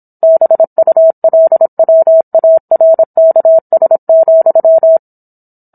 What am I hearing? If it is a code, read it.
Morse code, heard as a keyed tone, about 27 wpm, 650 Hz. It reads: BULWARKS,